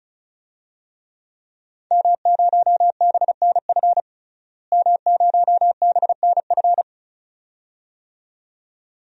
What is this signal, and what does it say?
Morse code, heard as a keyed tone, about 36 wpm, 700 Hz. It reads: M0BNF M0BNF
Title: DXCC-100 Call Signs - 5 Characters - Repeated 2x - 35wpm